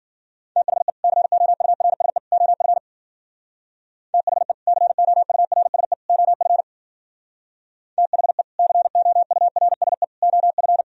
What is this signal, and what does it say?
Morse code, heard as a keyed tone, about 50 wpm, 700 Hz. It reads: THE COURSE OF THE COURSE OF THE COURSE OF
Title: Speed-Racing - Sets of 3 Words 40wpm